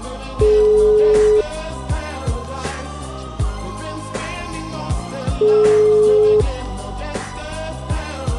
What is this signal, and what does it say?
Ringback tone (Europe, 425 Hz).